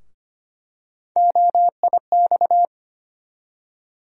Morse code, heard as a keyed tone, about 25 wpm, 700 Hz. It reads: OIX
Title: USA Call Sign Postfixes - 3 Letters 25wpm